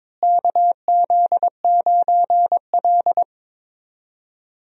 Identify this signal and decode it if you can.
Morse code, heard as a keyed tone, about 22 wpm, 700 Hz. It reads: KZ9L
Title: USA Call Signs - 4 Characters - Repeated 3x - 22wpm